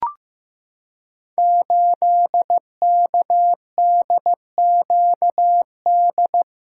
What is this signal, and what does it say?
Morse code, heard as a keyed tone, about 15 wpm, 700 Hz. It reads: 8KDQD
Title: Code Groups - 5 Letter:Numbers - 15wpm